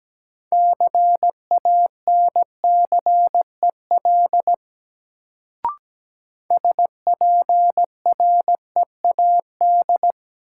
Morse code, heard as a keyed tone, about 17 wpm, 700 Hz. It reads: CANCEL SPREAD
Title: Sentences from Top 1200 Words - New Word Drill - 17wpm